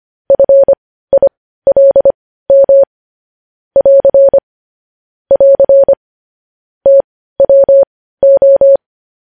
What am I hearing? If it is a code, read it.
Morse code, heard as a keyed tone, about 25 wpm, spaced as for 15 wpm, 550 Hz. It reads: FILM <AR> <AR> TWO
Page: News Headlines in Morse Code at 15 WPM